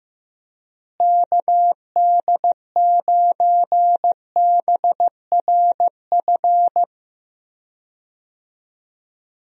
Morse code, heard as a keyed tone, about 15 wpm, 700 Hz. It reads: KD9BRF
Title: USA Call Signs - 6 Characters - Repeated 2x - 15wpm